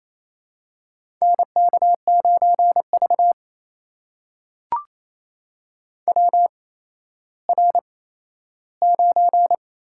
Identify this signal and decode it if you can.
Morse code, heard as a keyed tone, about 28 wpm, spaced as for 14 wpm, 700 Hz. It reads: NK9V WR9
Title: USA Call Signs - 4 Characters - 8x Character Spacing - 28wpm